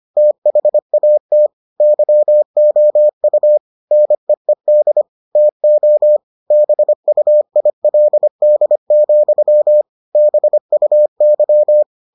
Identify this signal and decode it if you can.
Morse code, heard as a keyed tone, about 25 wpm, 600 Hz. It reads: THAT YOU NEED TO BUILD, BUY